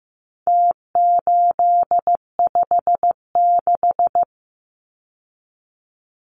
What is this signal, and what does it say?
Morse code, heard as a keyed tone, about 15 wpm, 700 Hz. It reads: T856
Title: Numbers - Contest Style - 4-Digit 15wpm